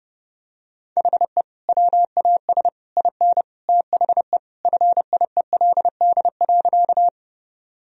Morse code, heard as a keyed tone, about 32 wpm, 700 Hz. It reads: HE WAS IN THE FIELD.
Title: Sentences from Top 1000 Words - Review Part 2 - 30wpm